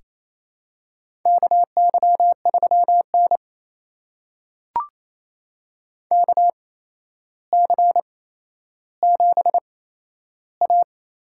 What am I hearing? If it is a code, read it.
Morse code, heard as a keyed tone, about 27 wpm, spaced as for 12 wpm, 700 Hz. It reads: KY3N KC7A
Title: USA Call Signs - 4 Characters - 8x Character Spacing - 28wpm